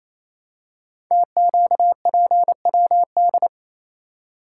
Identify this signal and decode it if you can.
Morse code, heard as a keyed tone, about 29 wpm, 700 Hz. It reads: TQPWD
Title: Code Groups - 5 Letters - 28wpm